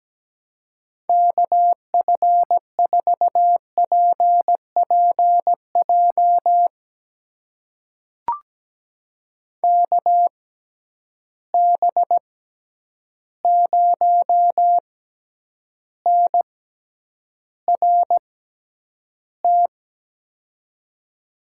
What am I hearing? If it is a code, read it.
Morse code, heard as a keyed tone, about 17 wpm, spaced as for 9 wpm, 700 Hz. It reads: KF4PPJ KB0NRT
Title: USA Call Signs - 6 Characters - 6x Character Spacing - 17wpm